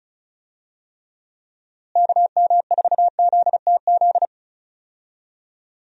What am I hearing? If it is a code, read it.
Morse code, heard as a keyed tone, about 36 wpm, 700 Hz. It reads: KM4ZTZ
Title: USA Call Signs - 6 Characters - Repeated 3x - 35wpm